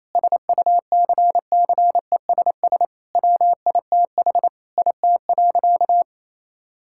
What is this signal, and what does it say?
Morse code, heard as a keyed tone, about 27 wpm, 700 Hz. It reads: SUCCESS WITH IT.
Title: Sentences from Top 1000 Words - Review Part 6 - 28wpm